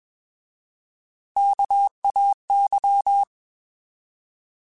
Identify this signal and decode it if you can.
Morse code, heard as a keyed tone, about 21 wpm, 800 Hz. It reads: KAY
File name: Alphabetical / Alphabetical mp3s / Ks @20